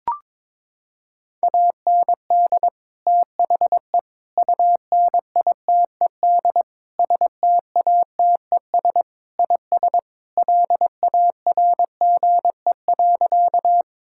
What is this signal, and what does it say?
Morse code, heard as a keyed tone, about 22 wpm, 700 Hz. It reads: AND THE UNITED STATES IS LARGE.
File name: practice / 22wpm / Sentences with 4-Word Phrases - 22wpm